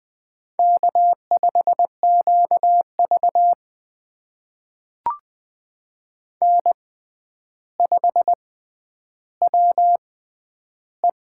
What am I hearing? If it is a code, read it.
Morse code, heard as a keyed tone, about 20 wpm, spaced as for 10 wpm, 700 Hz. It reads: K5QV N5WE
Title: USA Call Signs - 6x Character Spacing - 20wpm